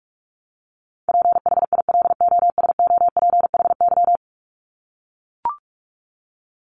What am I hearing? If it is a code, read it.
Morse code, heard as a keyed tone, about 44 wpm, 700 Hz. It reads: PHILOSOPHY
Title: Top 751-1000 Words 45wpm